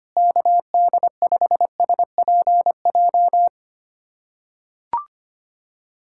Morse code, heard as a keyed tone, about 25 wpm, 700 Hz. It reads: KD5SPJ
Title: USA Callsign DE Callsign - 6 Characters - 25wpm